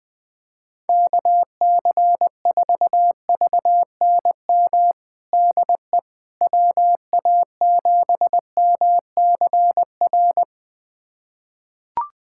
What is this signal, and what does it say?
Morse code, heard as a keyed tone, about 20 wpm, 700 Hz. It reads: KC4VNM DE WA7MCR
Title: USA Callsign DE Callsign - 6 Characters - 20wpm